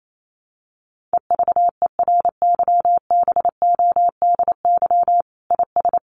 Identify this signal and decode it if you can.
Morse code, heard as a keyed tone, about 28 wpm, 700 Hz. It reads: EVERYBODY IS
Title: Sentences from Top 1400 Words 28wpm